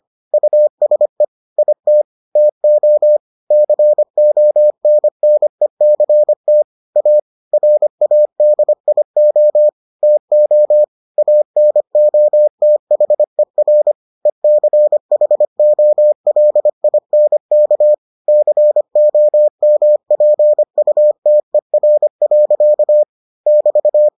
Morse code, heard as a keyed tone, about 25 wpm, 600 Hz. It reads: USE IT TO CONNECT A RADIO TO ANOTHER ECHOLINK COMPUTER. <BT>